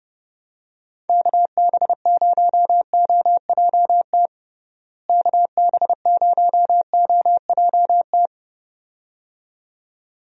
Morse code, heard as a keyed tone, about 31 wpm, 700 Hz. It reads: KB0OJT KB0OJT
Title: USA Call Signs - 6 Characters - Repeated 2x - 30wpm